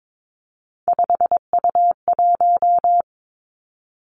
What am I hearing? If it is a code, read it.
Morse code, heard as a keyed tone, about 22 wpm, 700 Hz. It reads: 5U1
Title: Three Challenging Letter-Number Combinations 22wpm